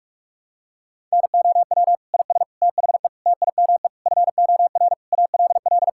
Morse code, heard as a keyed tone, about 46 wpm, 700 Hz. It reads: NOW IS THE TIME FOR ALL
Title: Pangram Challenge v2 - 45wpm